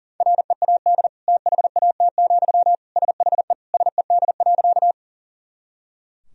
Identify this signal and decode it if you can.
Morse code, heard as a keyed tone, about 37 wpm, 700 Hz. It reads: READ THAT, SHE SED.
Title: Sentences from Top 200 Words and Abbreviations 40wpm